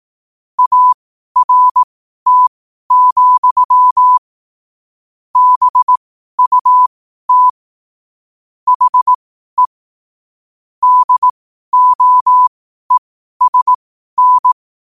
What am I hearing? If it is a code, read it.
Morse code, heard as a keyed tone, about 18 wpm, spaced as for 12 wpm, 1000 Hz. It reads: ART, BUT HE DOESN